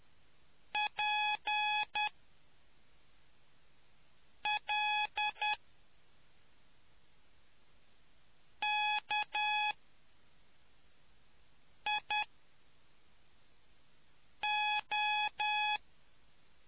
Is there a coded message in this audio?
Yes: Morse code from 0.7 to 15.8 s, heard as a keyed tone, about 10 wpm, spaced as for 3 wpm, 840 Hz. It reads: PLKIO